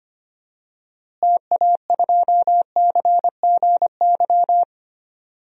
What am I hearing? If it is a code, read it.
Morse code, heard as a keyed tone, about 25 wpm, 700 Hz. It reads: TA2CGY
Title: DXCC-100 Call Signs - 6 Characters - Repeated 2x - 25wpm